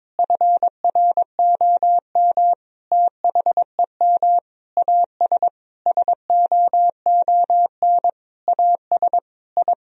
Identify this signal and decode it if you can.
Morse code, heard as a keyed tone, about 22 wpm, 700 Hz. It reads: FROM THEM AS SOON AS I
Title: Sentences from Top 1000 Words - Review Part 3 - 22wpm